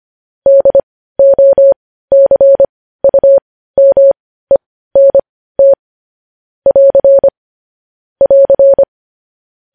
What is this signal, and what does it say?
Morse code, heard as a keyed tone, about 25 wpm, spaced as for 15 wpm, 550 Hz. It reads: DOCUMENT <AR> <AR>
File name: podcast / News headlines-1x-25x15